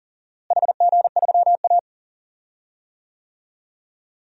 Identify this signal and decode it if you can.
Morse code, heard as a keyed tone, about 38 wpm, 700 Hz. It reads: HG3A
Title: DXCC-100 Call Signs - 4 Characters - Repeated 2x - 40wpm